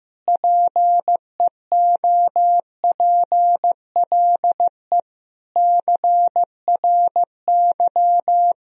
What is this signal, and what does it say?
Morse code, heard as a keyed tone, about 16 wpm, 700 Hz. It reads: PEOPLE CRY
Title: Sentences from Top 1000 Words - Review Part 2 - 15wpm